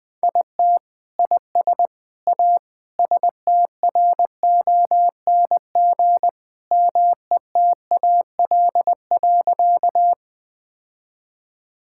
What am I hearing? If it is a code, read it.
Morse code, heard as a keyed tone, about 21 wpm, 700 Hz. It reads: IT IS A STRONG METAL.
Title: Sentences from Top 1000 Words - Review Part 6 - 20wpm